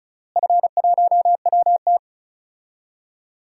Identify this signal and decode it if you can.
Morse code, heard as a keyed tone, about 34 wpm, 700 Hz. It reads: F1WT
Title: DXCC-100 Call Signs - 4 Characters - Repeated 3x - 35wpm